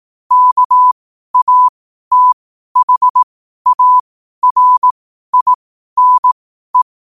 Morse code, heard as a keyed tone, about 18 wpm, spaced as for 12 wpm, 1000 Hz. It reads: KATHARINE
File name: qotdfiles / 12wpm